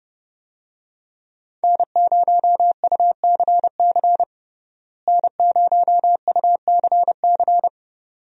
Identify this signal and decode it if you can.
Morse code, heard as a keyed tone, about 30 wpm, 700 Hz. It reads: N0UCC N0UCC
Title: USA Call Signs - 5 Characters - Repeated 2x - 30wpm